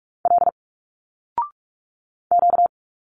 Morse code, heard as a keyed tone, about 46 wpm, 700 Hz. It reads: L Q
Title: Single Letters 45wpm